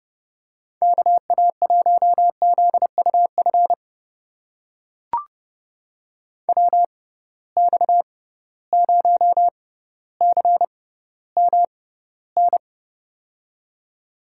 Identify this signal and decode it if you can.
Morse code, heard as a keyed tone, about 30 wpm, spaced as for 15 wpm, 700 Hz. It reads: KA1ZUF WX0CMN
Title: USA Call Signs - 6 Characters - 6x Character Spacing - 30wpm